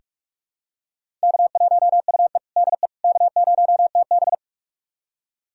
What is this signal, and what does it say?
Morse code, heard as a keyed tone, about 47 wpm, 700 Hz. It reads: K1UE DE K0TB
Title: USA Callsign DE Callsign - 4 Characters - 45wpm